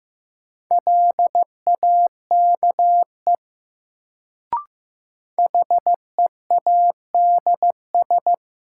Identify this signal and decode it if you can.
Morse code, heard as a keyed tone, about 16 wpm, 700 Hz. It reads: LAKE HEADS
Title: Sentences from Top 1000 Words - Review Part 3 - New Word Drill - 15wpm